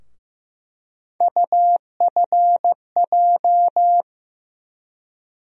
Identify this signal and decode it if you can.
Morse code, heard as a keyed tone, about 15 wpm, 700 Hz. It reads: UFJ
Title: USA Call Sign Postfixes - 3 Letters 15wpm